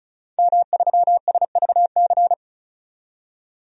Morse code, heard as a keyed tone, about 34 wpm, 700 Hz. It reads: M3SVC
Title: DXCC-100 Call Signs - 5 Characters - 35wpm